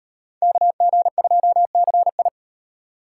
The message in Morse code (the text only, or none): KG2CI